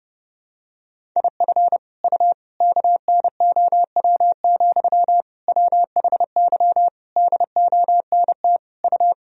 Morse code, heard as a keyed tone, about 32 wpm, 700 Hz. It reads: IF U KNOW, WHY DONT U
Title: Sentences from Top 300 Words and Abbreviations 30wpm